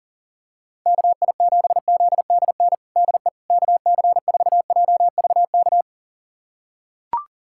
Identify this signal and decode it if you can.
Morse code, heard as a keyed tone, about 39 wpm, 700 Hz. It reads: KI7ZDN DE KC4JVK
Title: USA Callsign DE Callsign - 6 Characters - 40wpm